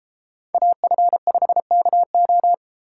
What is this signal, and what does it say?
Morse code, heard as a keyed tone, about 32 wpm, 700 Hz. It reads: AF5KO